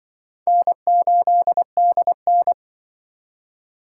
Morse code, heard as a keyed tone, about 24 wpm, 700 Hz. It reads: N8DN
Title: Speed Racing - USA 1x2 Call Signs 20wpm